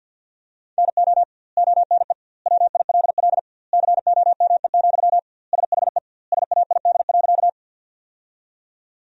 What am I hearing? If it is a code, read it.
Morse code, heard as a keyed tone, about 52 wpm, 700 Hz. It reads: NO ONE WILL COME, SHE SAID.
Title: 2x Word-Spacing - Sentences from Top 100 Words - 50wpm